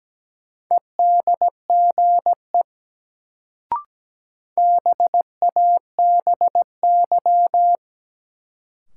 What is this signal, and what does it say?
Morse code, heard as a keyed tone, about 17 wpm, 700 Hz. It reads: EDGE BABY
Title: Sentences from Top 1300 Words - New Word Drill - 17wpm